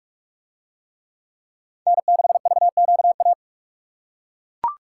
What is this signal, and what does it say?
Morse code, heard as a keyed tone, about 42 wpm, 700 Hz. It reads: N6VQA